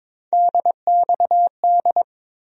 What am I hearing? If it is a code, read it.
Morse code, heard as a keyed tone, about 22 wpm, 700 Hz. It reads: DXD
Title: USA Call Sign Postfixes - 3 Letters 22wpm